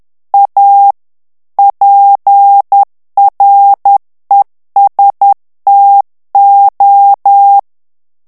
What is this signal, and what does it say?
Morse code, heard as a keyed tone, about 10 wpm, 800 Hz. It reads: A PRESTO
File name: morse mp3 / ITALIEN texte clair 1 vitesse 10 mots minute